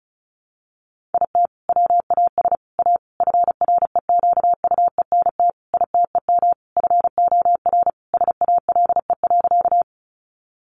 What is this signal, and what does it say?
Morse code, heard as a keyed tone, about 34 wpm, 700 Hz. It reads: IT WAS A FREQUENT ITEM FOR SALE.
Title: Sentences from Top 1400 Words 35wpm